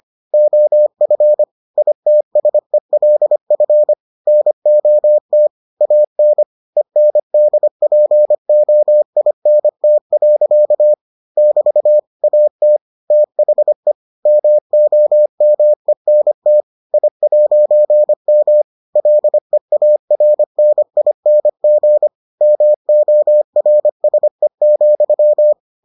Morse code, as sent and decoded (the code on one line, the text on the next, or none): --- ..-. / .. - ... .-.. ..-. / -. --- - / .- -. / . -. -.. .--. --- .. -. - .-.-.- / -...- / .- - / - .... / -- --- -- . -. - / .. .----. -- / .-.. .- .-. -. .. -. --. / -- --- .-. ... --..--
OF ITSELF NOT AN ENDPOINT. <BT> AT THE MOMENT I'M LEARNING MORSE,